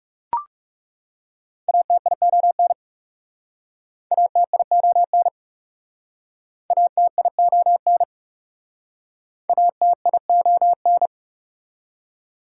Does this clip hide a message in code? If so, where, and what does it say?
Yes: Morse code from 1.7 to 11.1 s, heard as a keyed tone, about 40 wpm, 700 Hz. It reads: ATION ATION ATION ATION